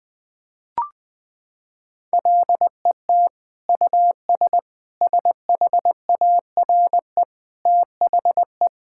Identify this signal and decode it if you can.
Morse code, heard as a keyed tone, about 21 wpm, 700 Hz. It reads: LET US SHARE THE
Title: Sentences from Top 1000 Words - Review Part 6 - 20wpm